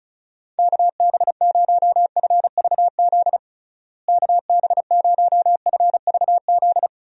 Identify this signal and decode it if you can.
Morse code, heard as a keyed tone, about 36 wpm, 700 Hz. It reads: KB0FVZ KB0FVZ
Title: USA Call Signs - 6 Characters - Repeated 3x - 35wpm